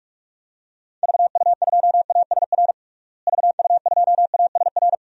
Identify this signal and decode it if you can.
Morse code, heard as a keyed tone, about 46 wpm, 700 Hz. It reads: VU2ASR VU2ASR
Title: DXCC-100 Call Signs - 6 Characters - Repeated 2x - 45wpm